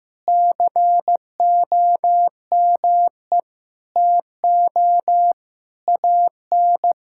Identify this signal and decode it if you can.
Morse code, heard as a keyed tone, about 16 wpm, 700 Hz. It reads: COME TO AN
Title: Sentences from Top 1000 Words - Review Part 5 - 15wpm